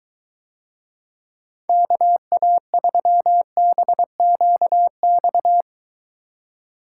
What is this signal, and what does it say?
Morse code, heard as a keyed tone, about 23 wpm, 700 Hz. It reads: KA3BQX